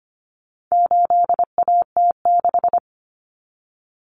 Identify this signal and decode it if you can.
Morse code, heard as a keyed tone, about 25 wpm, 700 Hz. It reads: 8AT6
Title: Common Cut Numbers - 4-Digit 25wpm